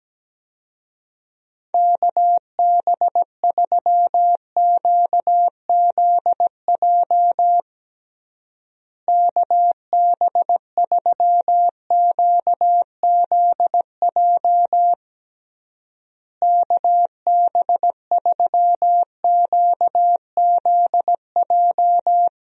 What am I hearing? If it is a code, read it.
Morse code, heard as a keyed tone, about 17 wpm, 700 Hz. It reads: KB3QZJ KB3QZJ KB3QZJ
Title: USA Call Signs - 6 Characters - Repeated 3x - 17wpm